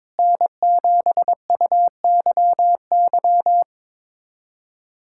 Morse code, heard as a keyed tone, about 21 wpm, 700 Hz. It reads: N7UYY
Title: USA Callsign DE Callsign 22wpm